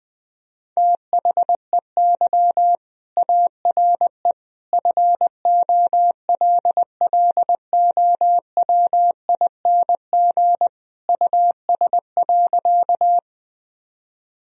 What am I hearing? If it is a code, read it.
Morse code, heard as a keyed tone, about 20 wpm, 700 Hz. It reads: THEY ARE FOLLOWING US.
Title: Sentences from Top 1000 Words - Review Part 4 - 20wpm